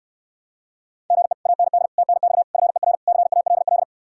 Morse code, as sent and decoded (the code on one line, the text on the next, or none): --. / .- -. -.. / - -. -..- / ..-. .-. / -.-. .- .-.. .-..
GE AND TNX FER CALL